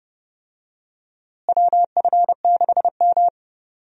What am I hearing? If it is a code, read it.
Morse code, heard as a keyed tone, about 32 wpm, 700 Hz. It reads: WF6M
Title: USA Call Signs - 4 Characters - Repeated 3x - 30wpm